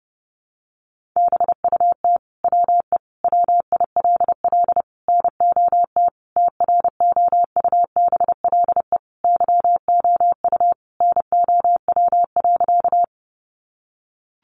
Morse code, heard as a keyed tone, about 30 wpm, 700 Hz. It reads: BUT WE WILL NOT TROUBLE YOU NOW.
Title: Sentences from Top 1200 Words 30wpm